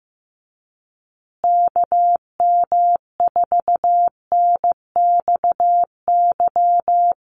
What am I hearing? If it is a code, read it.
Morse code, heard as a keyed tone, about 15 wpm, 700 Hz. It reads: KM4NXY
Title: USA Call Signs Repeated 2x 15wpm